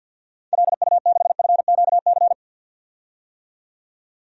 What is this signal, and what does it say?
Morse code, heard as a keyed tone, about 50 wpm, 700 Hz. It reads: LU6FQC
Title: DXCC-100 Call Signs - 6 Characters - 50wpm